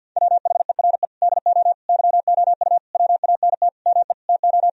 Morse code, heard as a keyed tone, about 51 wpm, 700 Hz. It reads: WHERE DO YOU WANT ME TO
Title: Sentences from Top 1000 Words - Review Part 2 - 50wpm